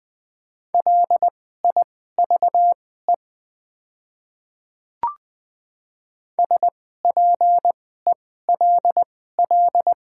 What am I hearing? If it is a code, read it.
Morse code, heard as a keyed tone, about 20 wpm, spaced as for 14 wpm, 700 Hz. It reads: LIVE SPELL
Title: 2x Character Spacing - Top 200 Words - 20wpm